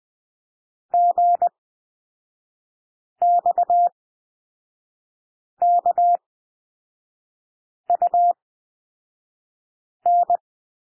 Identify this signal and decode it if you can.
Morse code, heard as a keyed tone, about 20 wpm, spaced as for 5 wpm, 700 Hz. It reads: GXKUN
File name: SOUND LETTERS 20 wpm